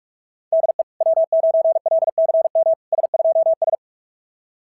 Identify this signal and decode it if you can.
Morse code, heard as a keyed tone, about 44 wpm, 660 Hz. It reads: DE W9LCM S2S
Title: SOTA QSO Practice 45wpm